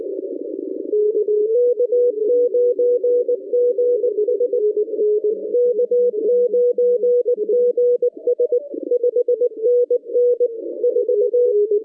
Morse code, heard as a keyed tone, about 19 wpm, 490 Hz. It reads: K9GS K9GS 5NN 4
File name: 2015 cq ww cw / 2015 cqwww k5tr - 0-20